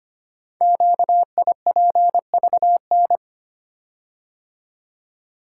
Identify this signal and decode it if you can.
Morse code, heard as a keyed tone, about 25 wpm, 700 Hz. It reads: QIPVN
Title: Code Groups - 5 Letter:Numbers - 25wpm